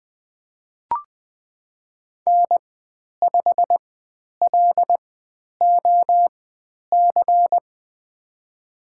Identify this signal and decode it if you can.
Morse code, heard as a keyed tone, about 20 wpm, spaced as for 10 wpm, 700 Hz. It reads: N5LOC
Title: 109 Lesson 18g - Letter m - Call Signs 20wpm